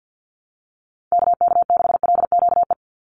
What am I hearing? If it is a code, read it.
Morse code, heard as a keyed tone, about 48 wpm, 700 Hz. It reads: KK6LQE
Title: USA Call Signs 50wpm